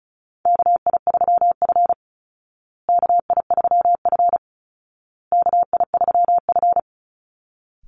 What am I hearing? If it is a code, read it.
Morse code, heard as a keyed tone, about 34 wpm, 700 Hz. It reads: KI3F KI3F KI3F
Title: Call Signs Repeated 2x - 35wpm